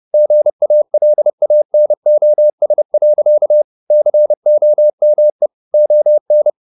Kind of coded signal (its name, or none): Morse code